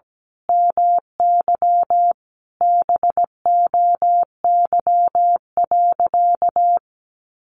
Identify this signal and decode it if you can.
Morse code, heard as a keyed tone, about 17 wpm, 700 Hz. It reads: MY BOY.
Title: Sentences from Top 800 Words 17wpm